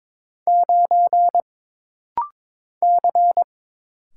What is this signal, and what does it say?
Morse code, heard as a keyed tone, about 22 wpm, 700 Hz. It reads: 9 C
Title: Single Letter-Number - Rapid-Fire 22wpm